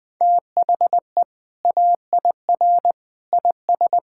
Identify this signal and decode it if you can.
Morse code, heard as a keyed tone, about 20 wpm, 700 Hz. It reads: THE AIR IS
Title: Sentences with 2-Word Phrases - 20wpm